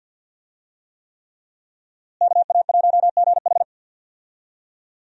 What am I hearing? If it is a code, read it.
Morse code, heard as a keyed tone, about 48 wpm, 700 Hz. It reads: KA1GH
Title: USA Call Signs - 5 Characters - Repeated 3x - 50wpm